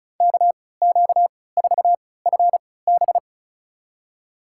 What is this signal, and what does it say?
Morse code, heard as a keyed tone, about 34 wpm, spaced as for 20 wpm, 700 Hz. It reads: KQ4FB
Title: USA Call Signs - 5 Characters - 3x Character Spacing - 35wpm